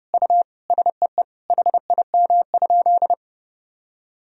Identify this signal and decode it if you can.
Morse code, heard as a keyed tone, about 30 wpm, 700 Hz. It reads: U SEE HIM?
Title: Sentences from Top 100 Words and Abbreviations 30wpm